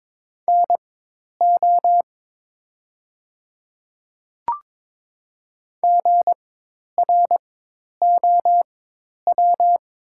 Morse code, heard as a keyed tone, about 22 wpm, spaced as for 10 wpm, 700 Hz. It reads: NO GROW
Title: 4x Character Spacing - Top 500 Words - 22wpm